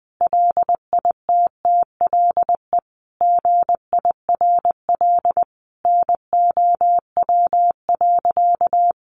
Morse code, heard as a keyed tone, about 20 wpm, 700 Hz. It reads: LITTLE GIRL NOW.
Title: Sentences from Top 1100 Words 20wpm